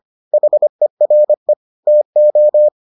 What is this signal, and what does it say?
Morse code, heard as a keyed tone, about 25 wpm, 600 Hz. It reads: HERE TO